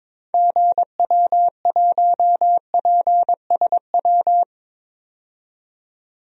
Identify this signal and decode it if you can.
Morse code, heard as a keyed tone, about 21 wpm, 700 Hz. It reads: GW1PSW
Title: DXCC-100 Call Signs - 6 Characters - 22wpm